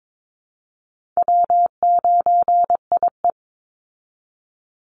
Morse code, heard as a keyed tone, about 22 wpm, 700 Hz. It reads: W9IE
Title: Easy Call Signs Repeated 3x - 22wpm